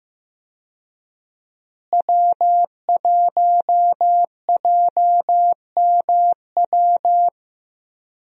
Morse code, heard as a keyed tone, about 15 wpm, 700 Hz. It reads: W1JMW
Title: USA Call Signs - 5 Characters - Repeated 3x - 15wpm